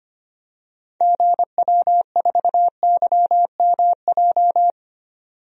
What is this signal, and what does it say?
Morse code, heard as a keyed tone, about 25 wpm, 700 Hz. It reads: GW4YMJ
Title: DXCC-100 Call Signs - 6 Characters - 25wpm